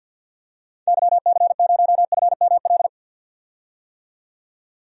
Morse code, heard as a keyed tone, about 53 wpm, 700 Hz. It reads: YC0FML
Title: DXCC-100 Call Signs - 6 Characters - 50wpm